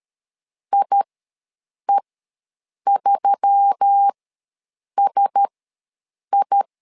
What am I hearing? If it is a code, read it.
Morse code, heard as a keyed tone, about 13 wpm, spaced as for 7 wpm, 800 Hz. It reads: IE3SI
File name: LEZIONI / 4 Q1S3 ws 12wpm 2038